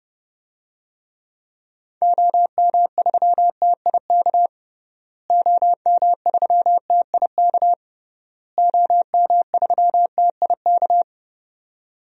Morse code, heard as a keyed tone, about 30 wpm, 700 Hz. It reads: OM3TIK OM3TIK OM3TIK
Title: DXCC-100 Call Signs - 6 Characters - Repeated 3x - 30wpm